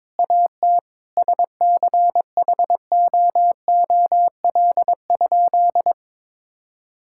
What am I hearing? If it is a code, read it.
Morse code, heard as a keyed tone, about 21 wpm, 700 Hz. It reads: AT SCHOOL?
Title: Sentences from Top 1000 Words - Review Part 6 - 22wpm